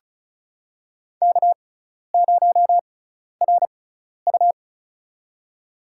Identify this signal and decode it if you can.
Morse code, heard as a keyed tone, about 36 wpm, spaced as for 12 wpm, 700 Hz. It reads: K0RU
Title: USA Call Signs - 6x Character Spacing - 35wpm